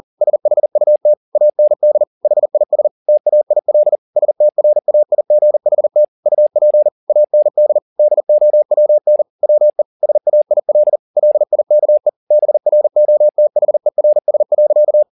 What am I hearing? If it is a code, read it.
Morse code, heard as a keyed tone, about 40 wpm, 600 Hz. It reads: SHUT AND HIS TAIL STRAIGHT UP AND DOWN WE SAIL LIKE BROTHERS.
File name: the old man and the sea 2 40wpm